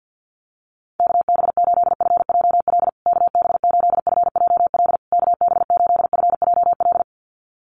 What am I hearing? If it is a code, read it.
Morse code, heard as a keyed tone, about 49 wpm, 700 Hz. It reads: KB8FJL KB8FJL KB8FJL
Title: USA Call Signs Repeated 3x 50wpm